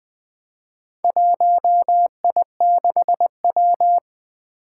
Morse code, heard as a keyed tone, about 21 wpm, 700 Hz. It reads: 1I6W
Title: Four Challenging Letter-Number Combinations 20wpm